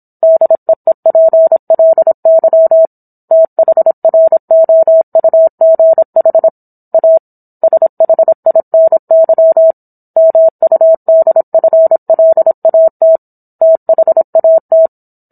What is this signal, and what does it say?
Morse code, heard as a keyed tone, about 27 wpm, 650 Hz. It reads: DEEPLY THROUGH A SHINY MUDFLAT THAT